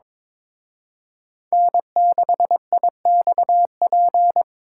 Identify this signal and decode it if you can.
Morse code, heard as a keyed tone, about 26 wpm, 700 Hz. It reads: N6IXP
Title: Speed Racing - USA 1x3 Call Signs 22wpm